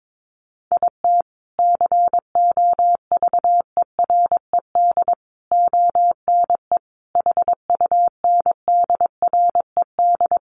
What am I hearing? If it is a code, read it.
Morse code, heard as a keyed tone, about 22 wpm, 700 Hz. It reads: IT COVERED ONE HUNDRED
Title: A Sound Of Thunder 22wpm